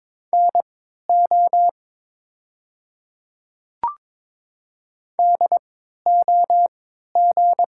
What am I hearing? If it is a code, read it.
Morse code, heard as a keyed tone, about 23 wpm, spaced as for 13 wpm, 700 Hz. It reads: NO DOG
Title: 3x Character Spacing - Top 500 Words - 22wpm